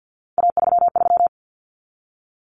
Morse code, heard as a keyed tone, about 50 wpm, 700 Hz. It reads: A33
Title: Common Cut Numbers - 3-Digit 50wpm